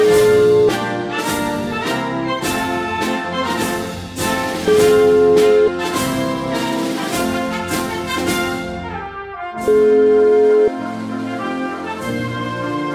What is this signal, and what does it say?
Ringback tone (Europe, 425 Hz), with noise.